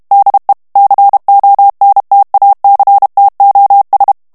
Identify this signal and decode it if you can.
Morse code, heard as a keyed tone, about 32 wpm, 800 Hz. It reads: DE CONTACTOS
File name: morse mp3 / ESPAGNOL TEXTE EN CLAIR 1 VITESSE 30 MOTS MINUTES